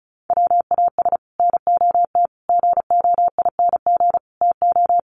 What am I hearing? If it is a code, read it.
Morse code, heard as a keyed tone, about 34 wpm, 700 Hz. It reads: WAS NOT GOING TO
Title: Sets of 5 Words Encore 35wpm